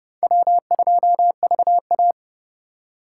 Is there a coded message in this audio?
Yes: Morse code from 0.2 to 2.1 s, heard as a keyed tone, about 29 wpm, 700 Hz. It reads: W2VA